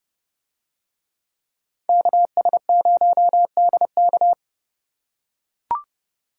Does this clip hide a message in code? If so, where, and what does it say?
Yes: Morse code from 1.9 to 4.3 s, heard as a keyed tone, about 30 wpm, 700 Hz. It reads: KS0DK